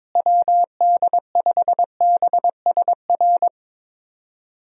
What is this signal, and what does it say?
Morse code, heard as a keyed tone, about 22 wpm, 700 Hz. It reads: WD5BSR
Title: USA Call Signs - 6 Characters - Repeated 3x - 22wpm